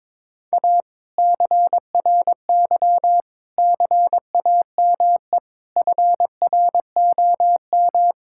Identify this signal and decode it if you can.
Morse code, heard as a keyed tone, about 21 wpm, 700 Hz. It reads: A CRY CAME FROM
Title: Sentences from Top 1000 Words - Review Part 5 - 22wpm